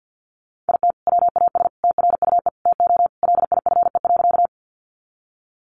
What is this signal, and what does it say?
Morse code, heard as a keyed tone, about 50 wpm, 700 Hz. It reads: IT WAS TRUE TO LIFE.
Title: Sentences from Top 400 Words 50wpm